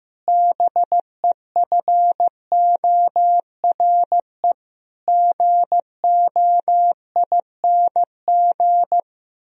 Morse code, heard as a keyed tone, about 15 wpm, 700 Hz. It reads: BEFORE GOING
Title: Sentences from Top 1000 Words - Review Part 3 - 15wpm